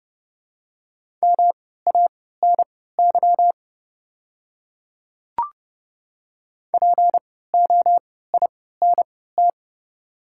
Morse code, heard as a keyed tone, about 30 wpm, spaced as for 17 wpm, 700 Hz. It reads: MANY POINT
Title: 3x Character Spacing - Top 300 Words - 30wpm